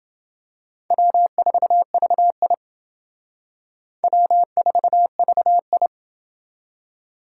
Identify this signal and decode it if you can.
Morse code, heard as a keyed tone, about 28 wpm, 700 Hz. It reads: W4VI W4VI